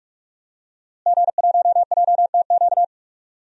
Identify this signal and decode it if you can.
Morse code, heard as a keyed tone, about 46 wpm, 700 Hz. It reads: G1JTQ
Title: DXCC-100 Call Signs - 5 Characters - Repeated 3x - 45wpm